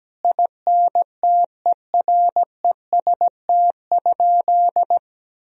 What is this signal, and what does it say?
Morse code, heard as a keyed tone, about 17 wpm, 700 Hz. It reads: INTEREST?
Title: Sentences from Top 1000 Words - Review Part 4 - 17wpm